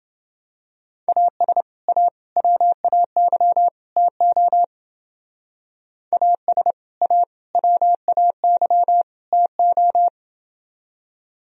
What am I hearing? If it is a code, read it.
Morse code, heard as a keyed tone, about 29 wpm, 700 Hz. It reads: AS A WAY TO AS A WAY TO